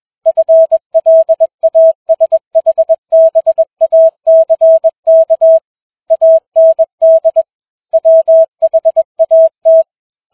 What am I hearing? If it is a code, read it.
Morse code, heard as a keyed tone, about 21 wpm, 650 Hz. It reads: FLASHBACK AND WHAT